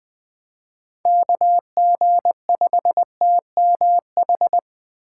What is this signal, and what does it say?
Morse code, heard as a keyed tone, about 20 wpm, 700 Hz. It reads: KG5TMH
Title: Speed Racing - USA 2x3 Call Signs 17wpm